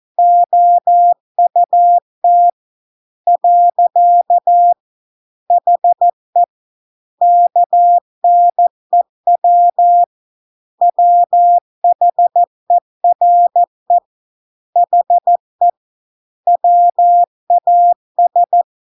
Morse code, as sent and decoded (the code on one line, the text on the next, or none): --- ..- - / .-.-.- / .... / -.- -. .-- / .-- .... .-. / .... / .-- .- ...
OUT . HE KNEW WHERE HE WAS